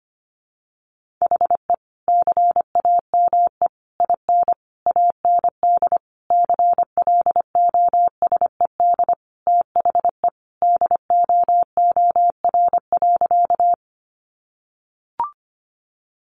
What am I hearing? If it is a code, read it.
Morse code, heard as a keyed tone, about 25 wpm, 700 Hz. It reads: HE CAME IN AND CLOSED THE DOOR.
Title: Sentences from Top 1400 Words 25wpm